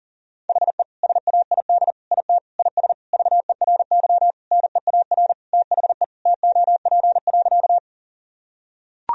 Morse code, heard as a keyed tone, about 40 wpm, 700 Hz. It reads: HE SAID IT IS VERY NEAR THE TOP.
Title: Sentences from Top 1000 Words - Review Part 2 - 40wpm